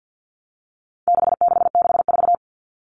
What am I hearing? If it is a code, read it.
Morse code, heard as a keyed tone, about 50 wpm, 700 Hz. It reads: 6664
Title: Numbers - 4-Digit 50wpm